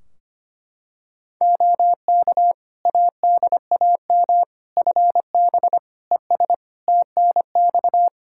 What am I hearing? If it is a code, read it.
Morse code, heard as a keyed tone, about 25 wpm, 700 Hz. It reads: OK ADAM FB ES TNX
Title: Practice QSO - Exchange 2 - Info - 25wpm